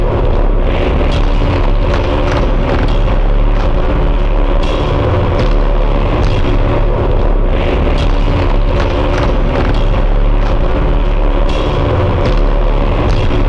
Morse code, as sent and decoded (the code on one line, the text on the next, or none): none